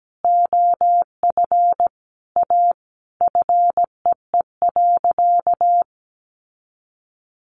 Morse code, heard as a keyed tone, about 17 wpm, 700 Hz. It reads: OF A FEE.
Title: Sentences from Top 1500 Words 17wpm